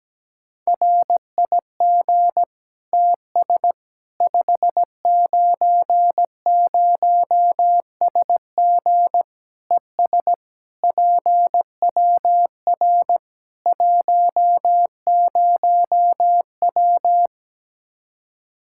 Morse code, heard as a keyed tone, about 17 wpm, 700 Hz. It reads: RIG TS 590SG ES PWR 10W
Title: Practice QSO - Exchange 2 - Info - 17wpm